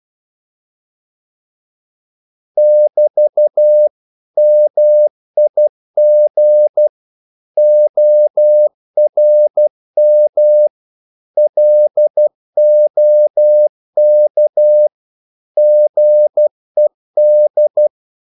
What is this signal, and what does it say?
Morse code, heard as a keyed tone, about 12 wpm, 600 Hz. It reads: <BT> MIG ORM LOK GED
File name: Korte ord DK 12wpm